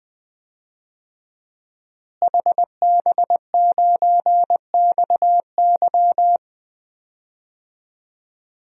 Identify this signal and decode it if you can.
Morse code, heard as a keyed tone, about 20 wpm, 700 Hz. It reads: HB9XY
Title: DXCC-100 Call Signs - 5 Characters - Repeated 3x - 20wpm